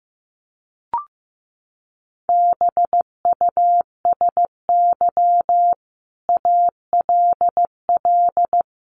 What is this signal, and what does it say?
Morse code, heard as a keyed tone, about 15 wpm, 700 Hz. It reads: BUSY ALL
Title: Sentences from Top 1500 Words 15wpm